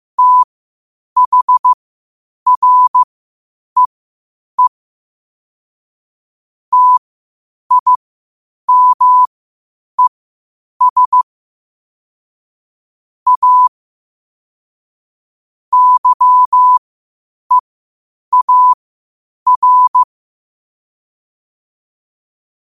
Morse code, heard as a keyed tone, about 15 wpm, spaced as for 8 wpm, 1000 Hz. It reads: THREE TIMES A YEAR
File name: 8wpm